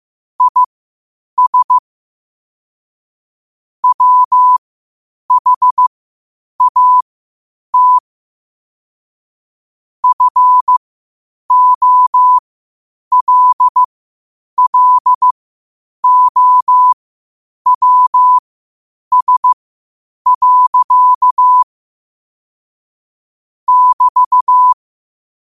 Morse code, heard as a keyed tone, about 15 wpm, spaced as for 8 wpm, 1000 Hz. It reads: IS WHAT FOLLOWS. <BT>
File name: qotdfiles / 8wpm